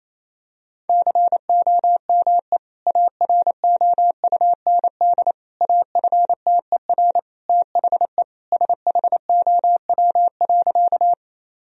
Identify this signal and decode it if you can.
Morse code, heard as a keyed tone, about 28 wpm, 700 Hz. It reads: COME AROUND AFTER THE SHOW.
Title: Sentences from Top 1000 Words - Review Part 3 - 28wpm